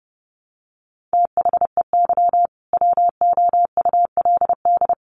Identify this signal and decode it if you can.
Morse code, heard as a keyed tone, about 30 wpm, 700 Hz. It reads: THEY WOULD
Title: Sentences from Top 500 Words 30wpm